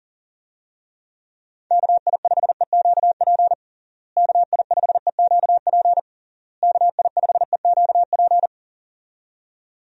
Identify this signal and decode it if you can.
Morse code, heard as a keyed tone, about 40 wpm, 700 Hz. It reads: KI5EQP KI5EQP KI5EQP
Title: USA Call Signs - 6 Characters - Repeated 3x - 40wpm